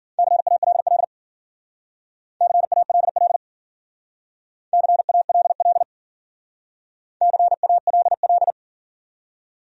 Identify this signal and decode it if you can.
Morse code, heard as a keyed tone, about 52 wpm, 700 Hz. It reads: CALL CALL CALL CALL